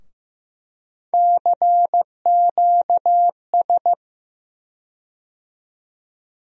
Morse code, heard as a keyed tone, about 15 wpm, 700 Hz. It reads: CQS
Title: USA Call Sign Postfixes - 3 Letters 15wpm